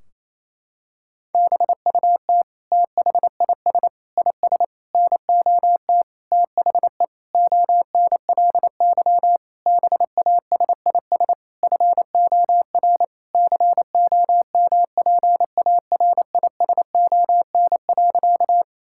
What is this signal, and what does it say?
Morse code, heard as a keyed tone, about 28 wpm, 700 Hz. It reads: BUT THIS IS NOT THE ONLY BASIS FOR COMPARISON.
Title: Sentences from Top 1600 Words 28wpm